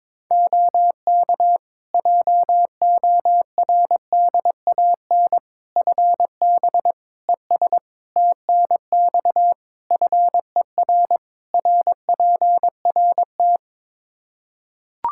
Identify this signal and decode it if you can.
Morse code, heard as a keyed tone, about 22 wpm, 700 Hz. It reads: OK JORDAN FB ES TNX FER RPRT
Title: Practice QSO - Exchange 2 - Info - 22wpm